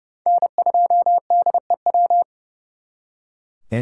Morse code, heard as a keyed tone, about 37 wpm, 700 Hz. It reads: N2DEW
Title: Speed Racing - USA Call Signs 30wpm